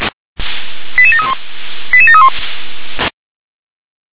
Five-tone selective call (ZVEI), heard as quick runs of five tones, tones of about 80 ms. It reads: 88731 88731
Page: /357/SDR/ZVEI/88712/